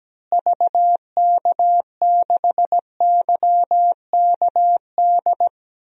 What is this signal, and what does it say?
Morse code, heard as a keyed tone, about 17 wpm, 700 Hz. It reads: VK6YKD
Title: DXCC-100 Call Signs - 6 Characters - Repeated 2x - 17wpm